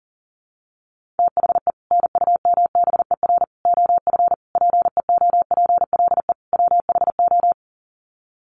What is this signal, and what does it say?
Morse code, heard as a keyed tone, about 40 wpm, 700 Hz. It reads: THE NUMBER OF PEOPLE WHO
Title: Sets of 5 Words Encore 40wpm